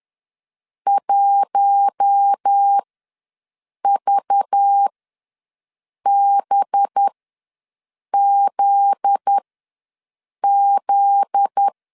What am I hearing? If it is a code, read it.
Morse code, heard as a keyed tone, about 10 wpm, spaced as for 6 wpm, 800 Hz. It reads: 1VBZZ
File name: LEZIONI / 9 2867 ws 10wpm 4215